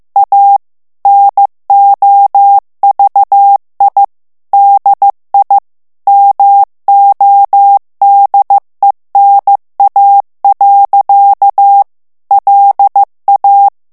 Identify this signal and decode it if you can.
Morse code, heard as a keyed tone, about 15 wpm, 800 Hz. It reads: A NOVI DI MODENA. LA
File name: morse mp3 / ITALIEN texte clair 1 vitesse 14 mots minute